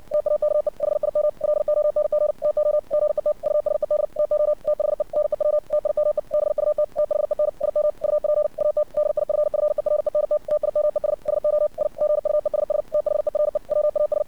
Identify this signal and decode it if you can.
Morse code, heard as a keyed tone, about 60 wpm, 600 Hz. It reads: TAKE HIM LONG TO GET USED TO THE NEW TIME BUT THEN IM UP AT DIFFERENT TIMES SO I GUESS THERE WAS
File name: archived / CRSnet